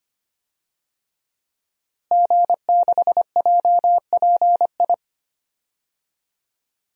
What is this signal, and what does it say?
Morse code, heard as a keyed tone, about 25 wpm, 700 Hz. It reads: G6JPI
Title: DXCC-100 Call Signs - 5 Characters - Repeated 3x - 25wpm